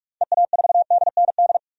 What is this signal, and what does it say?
Morse code, heard as a keyed tone, about 42 wpm, 700 Hz. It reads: EA4DND
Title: DXCC-100 Call Signs - 6 Characters - 45wpm